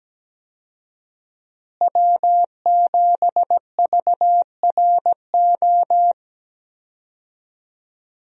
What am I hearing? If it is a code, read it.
Morse code, heard as a keyed tone, about 17 wpm, 700 Hz. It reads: W7VRO
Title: USA Call Signs - 5 Characters - Repeated 3x - 17wpm